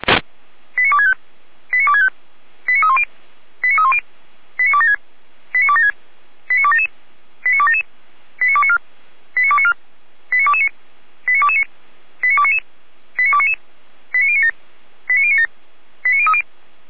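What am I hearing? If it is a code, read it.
Five-tone selective call (ZVEI), heard as quick runs of five tones, tones of about 70 ms. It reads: 89265 89265 89310 89310 89276 89276 89277 89277 89294 89294 89229 89229 89299 89299 89097 89097 89030
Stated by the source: ZVEI five-tone selective call 89160 received by SDR, logged by Feuerwehr Orlamünde fire brigade.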